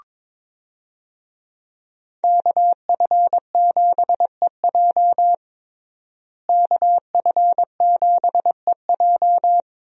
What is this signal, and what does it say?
Morse code, heard as a keyed tone, about 22 wpm, 700 Hz. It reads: KF7EJ KF7EJ
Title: USA Call Signs - 5 Characters - Repeated 3x - 22wpm